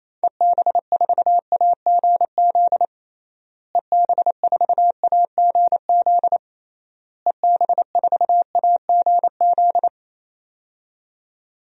Morse code, heard as a keyed tone, about 28 wpm, 700 Hz. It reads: EB4AGZ EB4AGZ EB4AGZ
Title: DXCC-100 Call Signs - 6 Characters - Repeated 3x - 28wpm